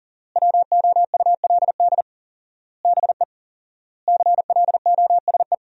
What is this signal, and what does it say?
Morse code, heard as a keyed tone, about 40 wpm, 700 Hz. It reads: WOULD BE CLOSE
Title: 4x Word-Spacing - Sentences from Top 300 Words - 40wpm